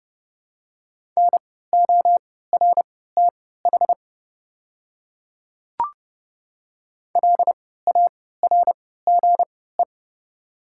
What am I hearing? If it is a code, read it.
Morse code, heard as a keyed tone, about 32 wpm, spaced as for 17 wpm, 700 Hz. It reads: NORTH LARGE
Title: 3x Character Spacing - Top 300 Words - 30wpm